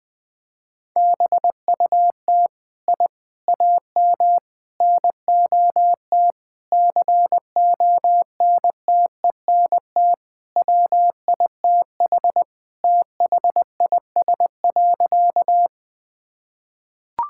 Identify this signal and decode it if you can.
Morse code, heard as a keyed tone, about 20 wpm, 700 Hz. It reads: BUT I AM NOT CONTENT WITH THIS.
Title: Sentences from Top 1000 Words - Review Part 6 - 20wpm